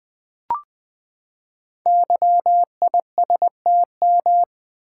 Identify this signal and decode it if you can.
Morse code, heard as a keyed tone, about 20 wpm, 700 Hz. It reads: YISTM
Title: Code Groups - 5 Letters - 20wpm